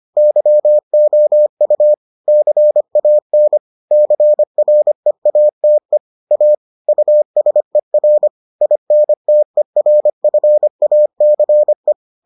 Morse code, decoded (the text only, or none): YOU CAN CREATE A USER INTERFACE